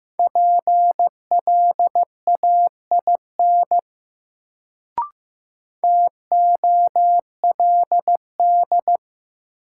Morse code, heard as a keyed tone, about 15 wpm, 700 Hz. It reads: PLAIN TOLD
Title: Sentences from Top 400 Words - New Word Drill - 15wpm